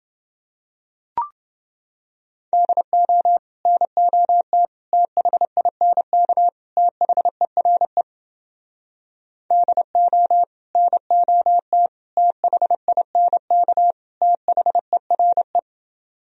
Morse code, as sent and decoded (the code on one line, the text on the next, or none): -.. --- / -. --- - / - .... .. -. -.- / - .... .-. / -.. --- / -. --- - / - .... .. -. -.- / - .... .-. .
DO NOT THINK THERE DO NOT THINK THERE